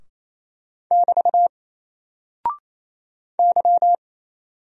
Morse code, heard as a keyed tone, about 27 wpm, 700 Hz. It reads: <BT> Y
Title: Single Character (Letter:Number:Punctuation:Prosign) 28wpm